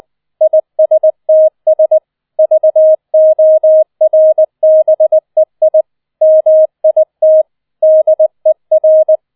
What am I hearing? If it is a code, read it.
Morse code, heard as a keyed tone, about 20 wpm, 630 Hz. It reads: ISTS VORBEI MIT DER
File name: 007 Letzter Streich 20wpm